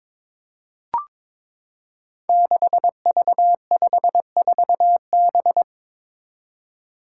Morse code, heard as a keyed tone, about 22 wpm, 700 Hz. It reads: 6V54B
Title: Five Challenging Letter-Number Combinations 22wpm